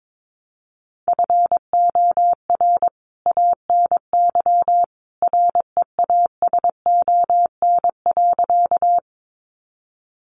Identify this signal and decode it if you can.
Morse code, heard as a keyed tone, about 22 wpm, 700 Hz. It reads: FOR ANY REASON.
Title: A Sound Of Thunder 22wpm